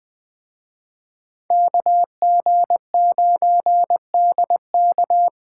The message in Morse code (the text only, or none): KG9DK